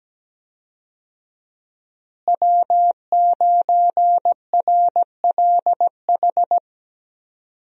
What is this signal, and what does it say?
Morse code, heard as a keyed tone, about 17 wpm, 700 Hz. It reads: W9RLH
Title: USA Call Signs - 5 Characters - Repeated 2x - 17wpm